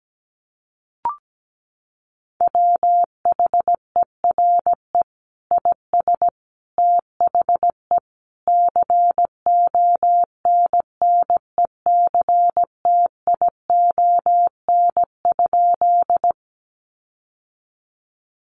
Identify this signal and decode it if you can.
Morse code, heard as a keyed tone, about 17 wpm, 700 Hz. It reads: WHERE IS THE CONNECTION?
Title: Sentences from Top 1300 Words 17wpm